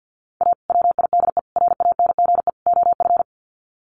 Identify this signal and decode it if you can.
Morse code, heard as a keyed tone, about 50 wpm, 700 Hz. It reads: A WIDE RANGE OF
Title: Sets of 4 Words 50wpm